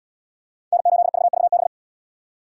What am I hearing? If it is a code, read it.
Morse code, heard as a keyed tone, about 76 wpm, 700 Hz. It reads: N8VVB